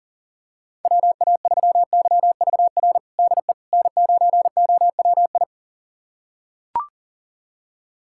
Morse code, heard as a keyed tone, about 40 wpm, 700 Hz. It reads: WA3YVR DE N9OWI
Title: USA Callsign DE Callsign 40wpm